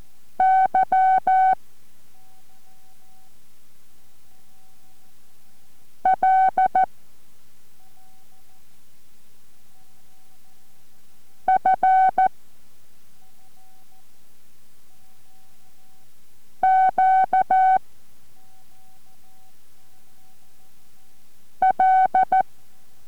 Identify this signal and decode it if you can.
Morse code, heard as a keyed tone, about 14 wpm, 760 Hz. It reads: Y L F Q L